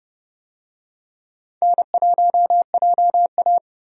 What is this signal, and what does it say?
Morse code, heard as a keyed tone, about 30 wpm, 700 Hz. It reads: N1JA
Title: USA Call Signs - 4 Characters - 30wpm